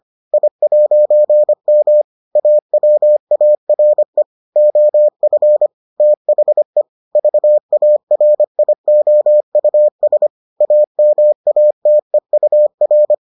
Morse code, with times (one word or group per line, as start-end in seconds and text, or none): 0.3-2.0 I'M
2.4-4.2 AWARE
4.6-5.7 OF
6.0-6.8 THE
7.2-10.3 VARIOUS
10.6-13.1 AMATEUR